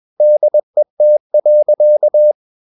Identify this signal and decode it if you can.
Morse code, heard as a keyed tone, about 21 wpm, 600 Hz. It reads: DET.
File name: Ferie QRS 21wpm